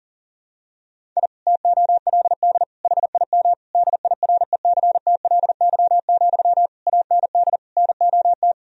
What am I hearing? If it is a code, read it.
Morse code, heard as a keyed tone, about 40 wpm, 700 Hz. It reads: I TOLD HIM DIRECTLY, AND NOT
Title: Sentences from Top 1000 Words - Review Part 1 - 40wpm